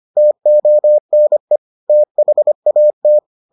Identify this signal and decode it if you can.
Morse code, heard as a keyed tone, about 25 wpm, 600 Hz. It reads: TONE THAT